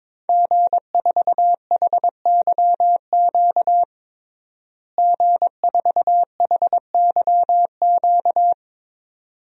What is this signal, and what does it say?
Morse code, heard as a keyed tone, about 22 wpm, 700 Hz. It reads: G4HYQ G4HYQ
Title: DXCC-100 Call Signs - 5 Characters - Repeated 2x - 22wpm